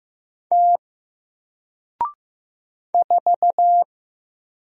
Morse code, heard as a keyed tone, about 15 wpm, 700 Hz. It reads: T 4
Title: Single Letter-Number Alt 15wpm